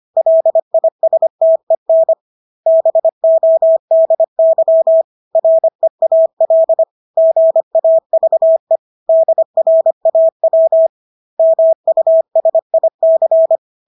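Morse code, heard as a keyed tone, about 25 wpm, 650 Hz. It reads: LISTEN BODY REAL GAVE DRAW MUSIC
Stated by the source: Morse practice of English words at 25 wpm from CW med Gnister.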